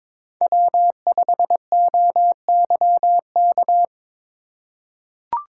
Morse code, heard as a keyed tone, about 22 wpm, 700 Hz. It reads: W5OYK
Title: USA Call Signs - 5 Characters - 2x Character Spacing - 22wpm